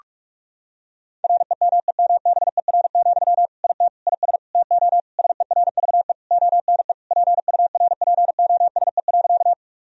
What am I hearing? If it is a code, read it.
Morse code, heard as a keyed tone, about 46 wpm, 700 Hz. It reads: REMEMBER, IT IS TO SERVE ONE PURPOSE.
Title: Sentences from Top 1000 Words - Review Part 2 - 45wpm